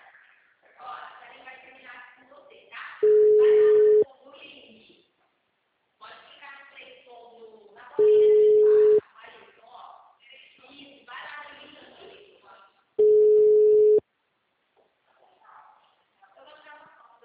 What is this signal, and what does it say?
Ringback tone (Europe, 425 Hz), with speech.
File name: out- -2016- - - .30548